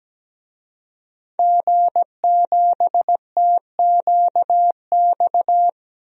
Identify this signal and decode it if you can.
Morse code, heard as a keyed tone, about 17 wpm, 700 Hz. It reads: G7TQX
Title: DXCC-100 Call Signs - 5 Characters - Repeated 3x - 17wpm